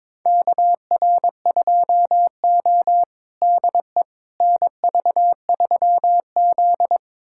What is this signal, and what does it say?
Morse code, heard as a keyed tone, about 22 wpm, 700 Hz. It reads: KR2O DE NV3Z
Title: USA Callsign DE Callsign - 4 Characters - 22wpm